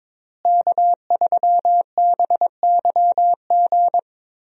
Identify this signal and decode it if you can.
Morse code, heard as a keyed tone, about 22 wpm, 700 Hz. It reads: K3BYG
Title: USA Call Signs - 5 Characters - Repeated 3x - 22wpm